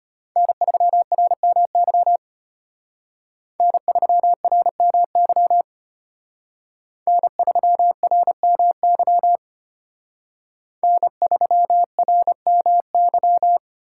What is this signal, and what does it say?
Morse code, heard as a keyed tone, about 32 wpm, 700 Hz. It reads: N3RMY N3RMY N3RMY N3RMY